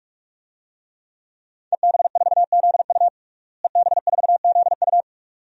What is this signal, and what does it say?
Morse code, heard as a keyed tone, about 44 wpm, 700 Hz. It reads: EB4ZU EB4ZU
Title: DXCC-100 Call Signs - 5 Characters - Repeated 2x - 45wpm